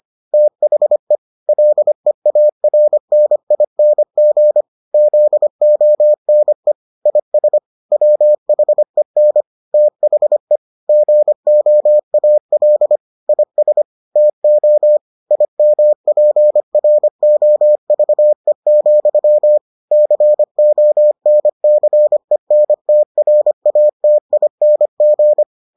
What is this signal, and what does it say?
Morse code, heard as a keyed tone, about 25 wpm, 600 Hz. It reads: THE LEARNING ZONE IS WHEN THE GOAL IS TO IMPROVE, CONCENTRATING